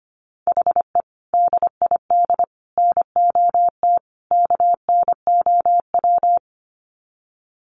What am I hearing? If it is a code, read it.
Morse code, heard as a keyed tone, about 25 wpm, 700 Hz. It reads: HE DID NOT KNOW
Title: Sets of 4 Words 25wpm